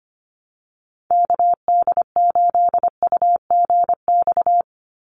Morse code, heard as a keyed tone, about 25 wpm, 700 Hz. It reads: KD8UGX
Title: USA Call Signs 25wpm